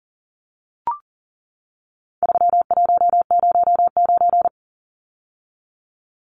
Morse code, heard as a keyed tone, about 40 wpm, 700 Hz. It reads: 3109
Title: Numbers - 4-Digit 40wpm